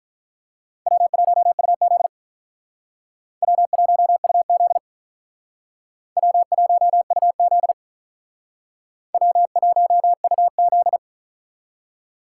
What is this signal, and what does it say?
Morse code, heard as a keyed tone, about 47 wpm, 700 Hz. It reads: W1UZ W1UZ W1UZ W1UZ